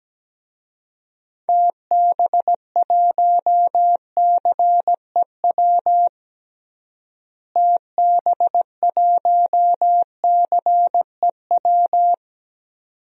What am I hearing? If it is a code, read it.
Morse code, heard as a keyed tone, about 17 wpm, 700 Hz. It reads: TB1CEW TB1CEW
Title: DXCC-100 Call Signs - 6 Characters - Repeated 2x - 17wpm